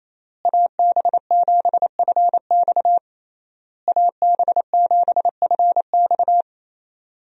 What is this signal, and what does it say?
Morse code, heard as a keyed tone, about 28 wpm, 700 Hz. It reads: AB7FX AB7FX
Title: USA Call Signs - 5 Characters - Repeated 3x - 28wpm